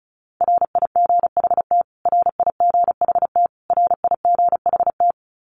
Morse code, heard as a keyed tone, about 34 wpm, 700 Hz. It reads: RIGHT RIGHT RIGHT
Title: Top 500 Words Repeated 35wpm